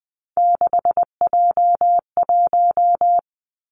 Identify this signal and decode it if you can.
Morse code, heard as a keyed tone, about 20 wpm, 700 Hz. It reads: 6J1
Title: Three Challenging Letter-Number Combinations 20wpm